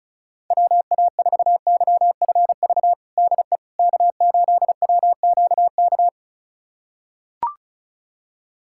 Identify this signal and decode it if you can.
Morse code, heard as a keyed tone, about 36 wpm, 700 Hz. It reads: WA4YFV DE K8WQK
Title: USA Callsign DE Callsign 35wpm